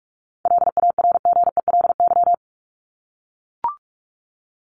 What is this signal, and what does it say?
Morse code, heard as a keyed tone, about 44 wpm, 700 Hz. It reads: LARGELY
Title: LY Suffix - Top 100 Words with Base Words 45wpm